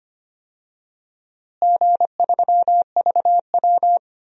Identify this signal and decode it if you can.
Morse code, heard as a keyed tone, about 25 wpm, 700 Hz. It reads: G3VW
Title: DXCC-100 Call Signs - 4 Characters - Repeated 2x - 25wpm